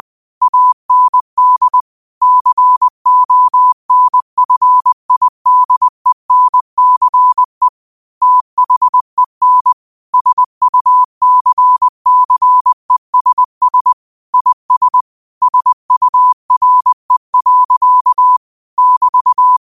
Morse code, heard as a keyed tone, about 20 wpm, 1000 Hz. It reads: AND CONFIDENCE THEN SUCCESS IS SURE. <BT>
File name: 20wpm